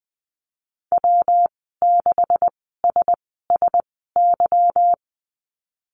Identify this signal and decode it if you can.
Morse code, heard as a keyed tone, about 20 wpm, spaced as for 15 wpm, 700 Hz. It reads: W6SSY
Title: 207 Lesson 33g - Letter j - Call Signs 20wpm